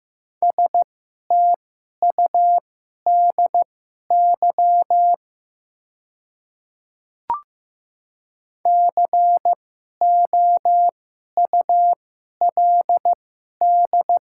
Morse code, heard as a keyed tone, about 15 wpm, spaced as for 11 wpm, 700 Hz. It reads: STUDY COULD
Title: 2x Character Spacing - Top 500 Words - 15wpm